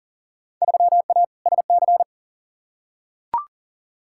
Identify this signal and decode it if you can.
Morse code, heard as a keyed tone, about 40 wpm, 700 Hz. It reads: 3A SC
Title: Field Day Exchanges - 40wpm